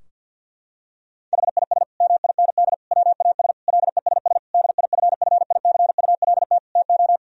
Morse code, heard as a keyed tone, about 46 wpm, 700 Hz. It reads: HIS MIND WAS LESS DIFFICULT TO
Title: Sentences from Top 1000 Words - Review Part 2 - 50wpm